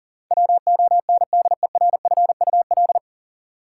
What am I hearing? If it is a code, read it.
Morse code, heard as a keyed tone, about 40 wpm, 700 Hz. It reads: WONDERFUL
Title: Sentences from Top 1100 Words - New Word Drill Alt - 40wpm